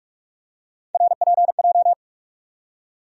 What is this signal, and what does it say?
Morse code, heard as a keyed tone, about 46 wpm, 700 Hz. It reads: RPJ
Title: USA Call Sign Postfixes - 3 Letters 45wpm